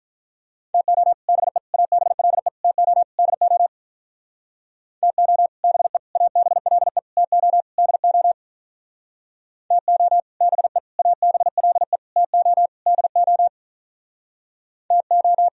Morse code, heard as a keyed tone, about 45 wpm, 700 Hz. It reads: TO BE ABLE TO DO TO BE ABLE TO DO TO BE ABLE TO DO TO